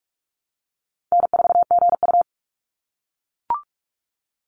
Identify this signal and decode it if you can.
Morse code, heard as a keyed tone, about 44 wpm, 700 Hz. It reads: N4GU
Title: Easy Call Signs - 45wpm